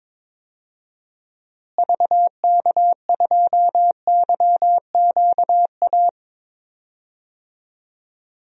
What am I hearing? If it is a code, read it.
Morse code, heard as a keyed tone, about 21 wpm, 700 Hz. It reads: VK2YQA
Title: DXCC-100 Call Signs - 6 Characters - Repeated 2x - 22wpm